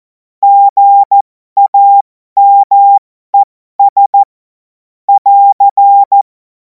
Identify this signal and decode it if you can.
Morse code, heard as a keyed tone, about 14 wpm, spaced as for 12 wpm, 800 Hz. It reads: GAMES <AR>
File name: Morsekurs / Witze